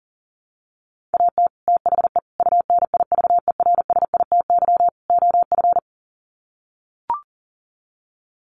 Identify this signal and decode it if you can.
Morse code, heard as a keyed tone, about 40 wpm, 700 Hz. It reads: AT THE UNIVERSITY OF
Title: Sets of 4 Words 40wpm